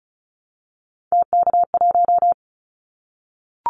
Morse code, heard as a keyed tone, about 36 wpm, 700 Hz. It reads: TK1
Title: Callsign Prefixes for Top 100 DXCC - 35wpm